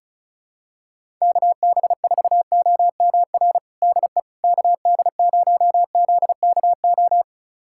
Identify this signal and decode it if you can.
Morse code, heard as a keyed tone, about 34 wpm, 700 Hz. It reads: KB4OMR DE KD0ZKO
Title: USA Callsign DE Callsign - 6 Characters - 35wpm